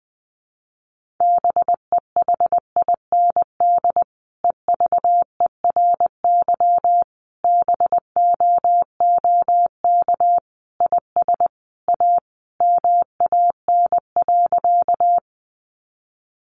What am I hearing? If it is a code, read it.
Morse code, heard as a keyed tone, about 20 wpm, 700 Hz. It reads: BEHIND EVERY BOOK IS A MAN.
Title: Sentences from Top 500 Words 20wpm